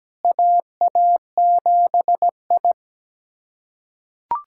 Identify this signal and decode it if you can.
Morse code, heard as a keyed tone, about 17 wpm, 700 Hz. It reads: AA7I
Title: USA Callsign DE Callsign - 4 Characters - 17wpm